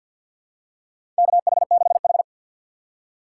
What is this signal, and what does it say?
Morse code, heard as a keyed tone, about 53 wpm, 700 Hz. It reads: KH6H
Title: USA Call Signs - 4 Characters - 2x Character Spacing - 50wpm